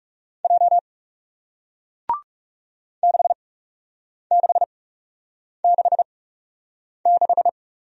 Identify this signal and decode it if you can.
Morse code, heard as a keyed tone, about 39 wpm, spaced as for 8 wpm, 700 Hz. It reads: J 6666